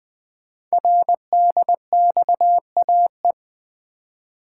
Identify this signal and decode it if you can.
Morse code, heard as a keyed tone, about 20 wpm, 700 Hz. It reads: RDXAE
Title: Code Groups - 5 Letters - 20wpm